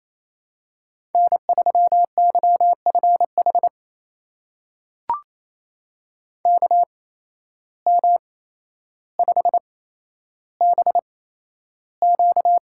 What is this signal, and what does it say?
Morse code, heard as a keyed tone, about 28 wpm, spaced as for 12 wpm, 700 Hz. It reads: N3YFH KM5BQ
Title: USA Call Signs - 5 Characters - 8x Character Spacing - 28wpm